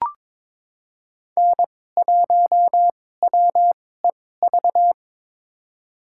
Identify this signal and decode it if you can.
Morse code, heard as a keyed tone, about 22 wpm, spaced as for 16 wpm, 700 Hz. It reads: N1WEV
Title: USA Call Signs - 5 Characters - 2x Character Spacing - 22wpm